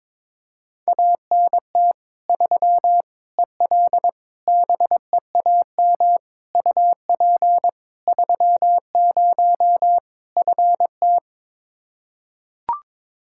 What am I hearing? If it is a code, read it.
Morse code, heard as a keyed tone, about 22 wpm, 700 Hz. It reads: ANT 3 EL BEAM UP 30 FT
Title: Practice QSO - Exchange 2 - Info - 22wpm